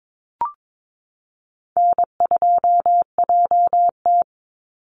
Morse code, heard as a keyed tone, about 22 wpm, 700 Hz. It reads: N2JT
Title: Easy Call Signs - 22wpm